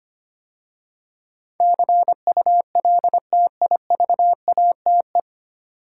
Morse code, heard as a keyed tone, about 25 wpm, 700 Hz. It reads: CULTIVATE
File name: practice / 25wpm / 9 Letter Words - First 3 Letters Given - 25wpm